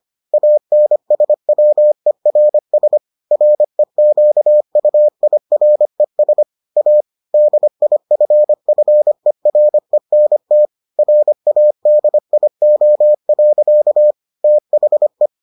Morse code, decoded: ANSWERS REQUIRES A DIFFERENT RADIO. THE

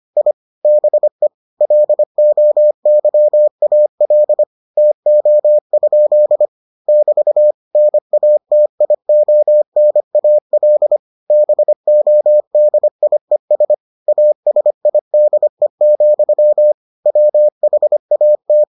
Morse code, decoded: I BE LOYAL TO? <BT> NATIONAL BODIES ASIDE, WHAT

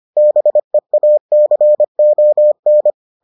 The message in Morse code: BEACON

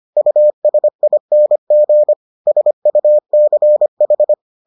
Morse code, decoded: USING SUCH

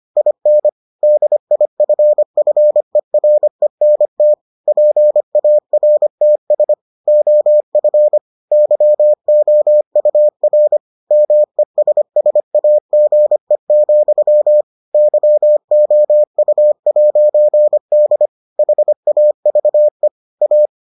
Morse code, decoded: IN DIFFERENT PARTS OF YOUR MESSAGE, YOU'D HAVE A

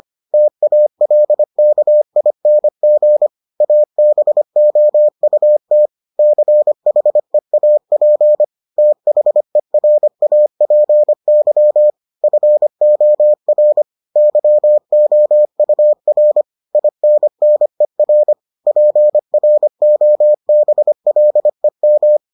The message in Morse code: TALKING ABOUT CHEAP THERAPY FOR YOUR INNER PROBLEM

